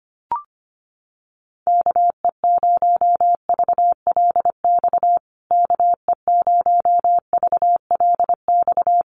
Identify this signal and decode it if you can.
Morse code, heard as a keyed tone, about 25 wpm, 700 Hz. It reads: KE0VLX KE0VLX
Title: USA Call Signs Repeated 3x 25wpm